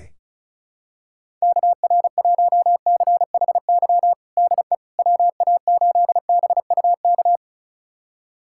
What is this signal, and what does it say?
Morse code, heard as a keyed tone, about 36 wpm, 700 Hz. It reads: KR1CHY DE WA8BUK
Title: USA Callsign DE Callsign - 6 Characters - 35wpm